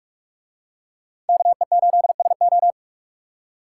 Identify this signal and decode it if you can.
Morse code, heard as a keyed tone, about 46 wpm, 700 Hz. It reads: KE8SO